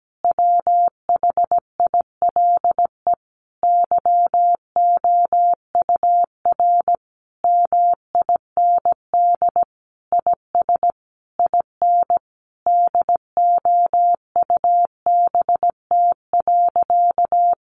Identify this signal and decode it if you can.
Morse code, heard as a keyed tone, about 17 wpm, 700 Hz. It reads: WHILE YOUR MIND IS IN DOUBT.
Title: Sentences from Top 1500 Words 17wpm